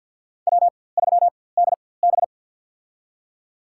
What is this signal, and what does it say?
Morse code, heard as a keyed tone, about 53 wpm, spaced as for 24 wpm, 700 Hz. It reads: W3DB
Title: USA Call Signs - 4x Character Spacing - 50wpm